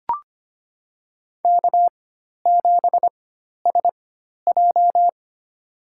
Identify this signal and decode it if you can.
Morse code, heard as a keyed tone, about 25 wpm, spaced as for 12 wpm, 700 Hz. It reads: K7SJ
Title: USA Call Signs - 4 Characters - 4x Character Spacing - 25wpm